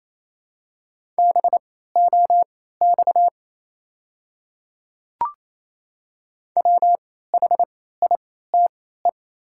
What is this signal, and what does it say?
Morse code, heard as a keyed tone, about 28 wpm, spaced as for 16 wpm, 700 Hz. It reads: BOX WHITE
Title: 3x Character Spacing - Top 500 Words - 28wpm